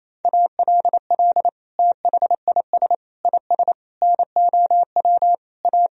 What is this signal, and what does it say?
Morse code, heard as a keyed tone, about 28 wpm, 700 Hz. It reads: ALL THIS IS NOW A
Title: Sentences from Top 1000 Words - Review Part 3 - 28wpm